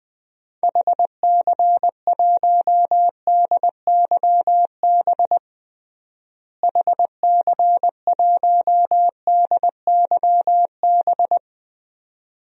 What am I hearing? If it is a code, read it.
Morse code, heard as a keyed tone, about 20 wpm, 700 Hz. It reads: HC1DYB HC1DYB
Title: DXCC-100 Call Signs - 6 Characters - Repeated 2x - 20wpm